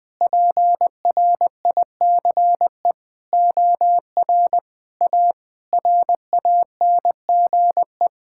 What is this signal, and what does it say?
Morse code, heard as a keyed tone, about 21 wpm, 700 Hz. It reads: PRICE OR A RANGE
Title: Sentences with 9-Word Phrases - 20wpm